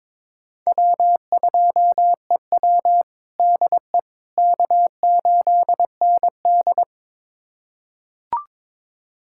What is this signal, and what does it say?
Morse code, heard as a keyed tone, about 22 wpm, 700 Hz. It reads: W2EW DE K8ND
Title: USA Callsign DE Callsign - 4 Characters - 22wpm